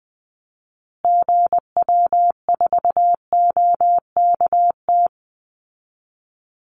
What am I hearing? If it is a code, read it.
Morse code, heard as a keyed tone, about 20 wpm, 700 Hz. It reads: GW4OKT
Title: Call Signs Repeated 2x - 20wpm